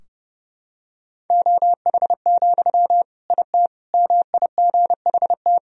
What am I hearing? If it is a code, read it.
Morse code, heard as a keyed tone, about 30 wpm, 700 Hz. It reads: OH, IT MIGHT
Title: Sentences from Top 1000 Words - Review Part 2 - 30wpm